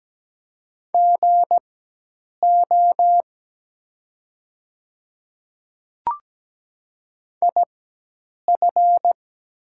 Morse code, heard as a keyed tone, about 17 wpm, spaced as for 8 wpm, 700 Hz. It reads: GO IF
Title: 4x Character Spacing - Top 100 Words - 17wpm